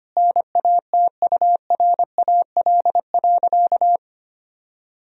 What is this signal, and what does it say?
Morse code, heard as a keyed tone, about 25 wpm, 700 Hz. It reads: NATURAL.
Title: Sentences from Top 1000 Words - Review Part 4 - 25wpm